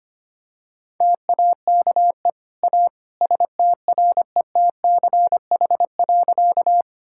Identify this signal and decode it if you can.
Morse code, heard as a keyed tone, about 25 wpm, 700 Hz. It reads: TAKE A STRETCH.
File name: Easy Sentences with Hints - 25wpm